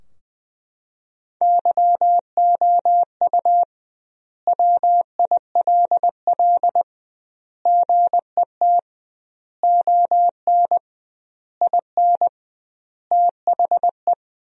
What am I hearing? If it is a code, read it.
Morse code, heard as a keyed tone, about 20 wpm, 700 Hz. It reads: YOU WILL GET ON IN THE
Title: 2x Word-Spacing - Sentences from Top 200 Words - 20wpm